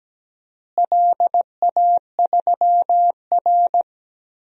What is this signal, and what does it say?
Morse code, heard as a keyed tone, about 17 wpm, 700 Hz. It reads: LA3R
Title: DXCC-100 Call Signs - 4 Characters - Repeated 3x - 17wpm